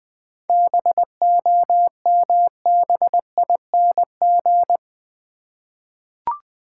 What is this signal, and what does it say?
Morse code, heard as a keyed tone, about 21 wpm, 700 Hz. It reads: BOMBING
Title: 7 Letter Words - First 3 Letters Given - 20wpm